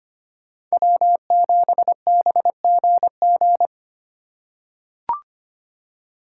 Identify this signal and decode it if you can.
Morse code, heard as a keyed tone, about 25 wpm, 700 Hz. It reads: W7BGG
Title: USA Call Signs - 5 Characters - 2x Character Spacing - 25wpm